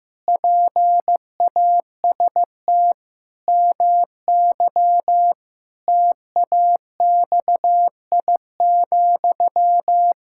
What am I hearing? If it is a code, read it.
Morse code, heard as a keyed tone, about 15 wpm, 700 Hz. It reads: PAST MY TAXI,
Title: Pangram Challenge v2 - 15wpm